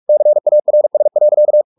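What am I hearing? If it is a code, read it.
Morse code, heard as a keyed tone, about 44 wpm, 600 Hz. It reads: CARS.
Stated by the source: Morse practice at 45 wpm, the novel Never by Ken Follett in English, from CW med Gnister.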